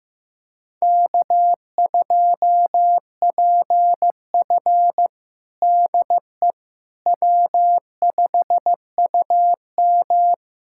Morse code, heard as a keyed tone, about 15 wpm, 700 Hz. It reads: K2PF DE W5UM
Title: USA Callsign DE Callsign - 4 Characters - 15wpm